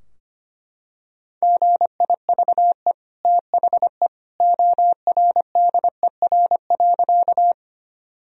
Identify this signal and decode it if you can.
Morse code, heard as a keyed tone, about 25 wpm, 700 Hz. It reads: GIVE THE ORDER.
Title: Sentences from Top 1000 Words - Review Part 2 - 25wpm